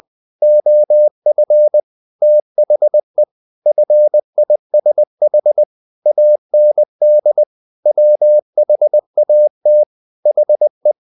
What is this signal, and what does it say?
Morse code, heard as a keyed tone, about 20 wpm, 600 Hz. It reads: OF THE FISH AND WHAT HE
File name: the old man and the sea part 2 20wpm